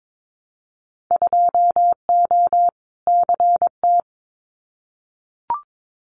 Morse code, heard as a keyed tone, about 22 wpm, 700 Hz. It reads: 2O CT
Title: Winter Field Day Exchanges 22wpm